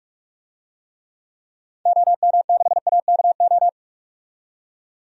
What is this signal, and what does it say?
Morse code, heard as a keyed tone, about 46 wpm, 700 Hz. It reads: OM6AKO
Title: DXCC-100 Call Signs - 6 Characters - 45wpm